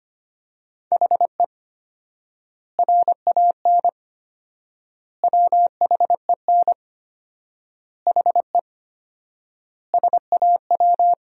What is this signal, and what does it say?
Morse code, heard as a keyed tone, about 25 wpm, 700 Hz. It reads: HE RAN WHEN HE SAW
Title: 4x Word-Spacing - Sentences from Top 500 Words - 25wpm